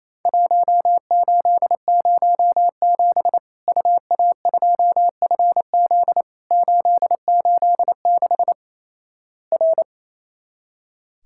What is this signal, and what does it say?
Morse code, heard as a keyed tone, about 28 wpm, 700 Hz. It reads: 1807 UA2FZ 886 R
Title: Worked All Europe DX Contest - QTC Exchange - 28wpm